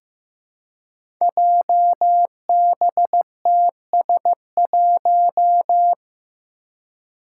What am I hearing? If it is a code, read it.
Morse code, heard as a keyed tone, about 15 wpm, 700 Hz. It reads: JBTS1
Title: Five Challenging Letter-Number Combinations 15wpm